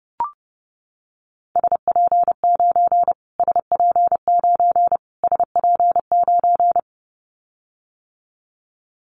Morse code, heard as a keyed tone, about 30 wpm, 700 Hz. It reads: SP9 SP9 SP9
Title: Callsign Prefixes for Top 25 DXCC - Repeated 3x - 30wpm